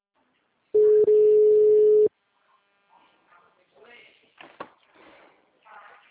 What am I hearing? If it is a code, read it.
Ringback tone (Europe, 425 Hz).